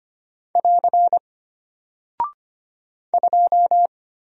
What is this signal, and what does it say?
Morse code, heard as a keyed tone, about 25 wpm, 700 Hz. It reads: <AR> 2
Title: Single Character (Letter:Number:Punctuation:Prosign) Alt 25wpm